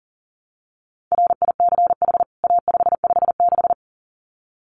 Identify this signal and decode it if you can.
Morse code, heard as a keyed tone, about 40 wpm, 700 Hz. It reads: RICH A556
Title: CWT Exchange 40wpm